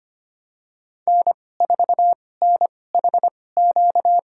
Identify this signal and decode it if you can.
Morse code, heard as a keyed tone, about 25 wpm, spaced as for 18 wpm, 700 Hz. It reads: N4NHQ
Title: USA Call Signs - 5 Characters - 2x Character Spacing - 25wpm